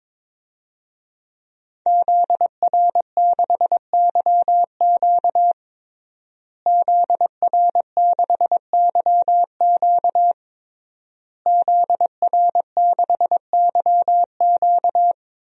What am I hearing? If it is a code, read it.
Morse code, heard as a keyed tone, about 22 wpm, 700 Hz. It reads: ZR6YQ ZR6YQ ZR6YQ
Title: DXCC-100 Call Signs - 5 Characters - Repeated 3x - 22wpm